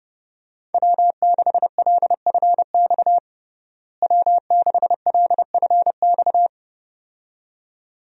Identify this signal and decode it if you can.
Morse code, heard as a keyed tone, about 30 wpm, 700 Hz. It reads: W6LFX W6LFX
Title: USA Call Signs - 5 Characters - Repeated 2x - 30wpm